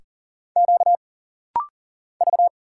Morse code, heard as a keyed tone, about 40 wpm, 700 Hz. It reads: Q V
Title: Single Letter-Number Alt - Rapid-Fire 40wpm